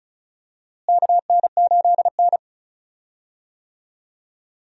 Morse code, heard as a keyed tone, about 36 wpm, 700 Hz. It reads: KN8N
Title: USA Call Signs - 4 Characters - Repeated 2x - 35wpm